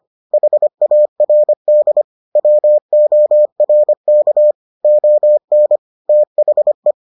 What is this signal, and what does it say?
Morse code, heard as a keyed tone, about 25 wpm, 600 Hz. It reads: HARD WORK ON THE